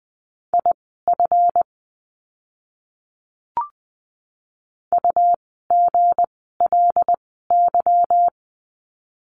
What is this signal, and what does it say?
Morse code, heard as a keyed tone, about 20 wpm, spaced as for 14 wpm, 700 Hz. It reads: IF UGLY
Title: 160 Lesson 26f - Letter g - Words 20wpm